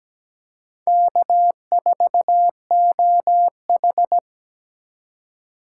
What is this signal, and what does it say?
Morse code, heard as a keyed tone, about 17 wpm, 700 Hz. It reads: K4OH
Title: USA Call Signs - 4 Characters - 2x Character Spacing - 17wpm